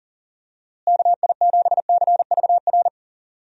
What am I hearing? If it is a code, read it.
Morse code, heard as a keyed tone, about 40 wpm, 700 Hz. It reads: KI7CVR
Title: USA Call Signs - 6 Characters - Repeated 2x - 40wpm